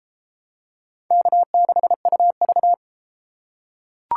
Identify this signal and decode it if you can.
Morse code, heard as a keyed tone, about 30 wpm, 700 Hz. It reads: K6UV